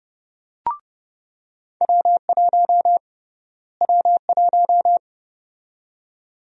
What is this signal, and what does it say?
Morse code, heard as a keyed tone, about 30 wpm, 700 Hz. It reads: W1 W1
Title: USA Call Sign Prefixes - Repeated 2x - 30wpm